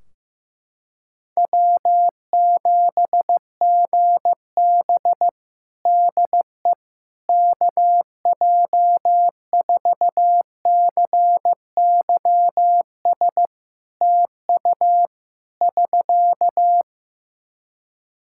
Morse code, heard as a keyed tone, about 15 wpm, 700 Hz. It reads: W7GB DE KJ4CYS TU <SK>
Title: Practice QSO - Exchange 4 - Ending - 15wpm